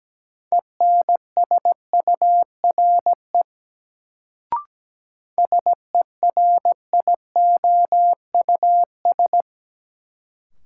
Morse code, heard as a keyed tone, about 17 wpm, 700 Hz. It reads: ENSURE SERIOUS
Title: Sentences from Top 1100 Words - New Word Drill Alt - 17wpm